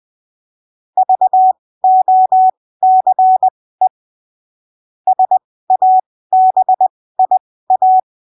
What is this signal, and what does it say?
Morse code, heard as a keyed tone, about 20 wpm, spaced as for 15 wpm, 750 Hz. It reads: VOCE SABIA
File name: arquivos / texto / texto 17 20ppm